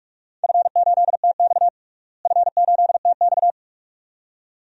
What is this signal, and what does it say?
Morse code, heard as a keyed tone, about 47 wpm, 700 Hz. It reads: F8TX F8TX
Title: DXCC-100 Call Signs - 4 Characters - Repeated 2x - 45wpm